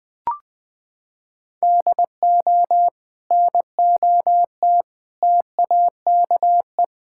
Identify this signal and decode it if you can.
Morse code, heard as a keyed tone, about 20 wpm, 700 Hz. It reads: DO NOT TAKE
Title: Sentences from Top 1600 Words 20wpm